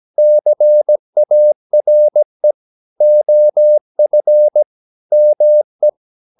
Morse code, heard as a keyed tone, about 17 wpm, 600 Hz. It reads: CARE OF ME
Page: Romanen Never af Ken Follet på engelsk.